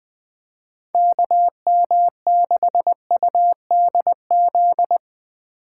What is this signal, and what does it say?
Morse code, heard as a keyed tone, about 20 wpm, 700 Hz. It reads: KM6UDZ
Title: USA Call Signs - 8x Character Spacing - 20wpm